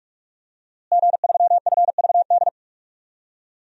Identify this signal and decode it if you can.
Morse code, heard as a keyed tone, about 44 wpm, 700 Hz. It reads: G3FVD
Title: DXCC-100 Call Signs - 5 Characters - Repeated 3x - 45wpm